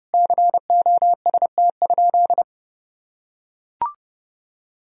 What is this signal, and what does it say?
Morse code, heard as a keyed tone, about 30 wpm, 700 Hz. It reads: COST?
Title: Sentences from Top 1000 Words - Review Part 4 - 30wpm